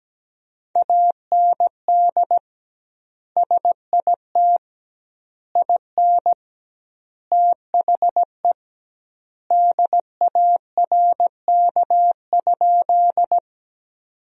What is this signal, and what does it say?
Morse code, heard as a keyed tone, about 17 wpm, 700 Hz. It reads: AND SIT IN THE DARK?
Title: 2x Word-Spacing - Sentences from Top 500 Words - 17wpm